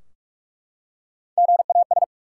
Morse code, heard as a keyed tone, about 44 wpm, 700 Hz. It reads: GAS
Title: Three Letter Words - 45wpm